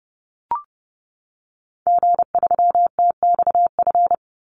Morse code, heard as a keyed tone, about 30 wpm, 700 Hz. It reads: G3TXF
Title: Call Signs Repeated 3x - 30wpm